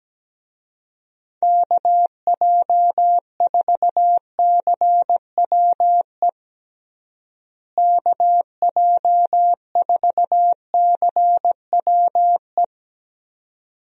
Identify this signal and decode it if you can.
Morse code, heard as a keyed tone, about 17 wpm, 700 Hz. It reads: KJ4CWE KJ4CWE
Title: USA Call Signs - 6 Characters - Repeated 2x - 17wpm